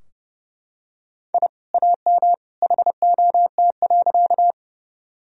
Morse code, heard as a keyed tone, about 30 wpm, 700 Hz. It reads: I AM HOT.
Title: Sentences from Top 1000 Words - Review Part 3 - 30wpm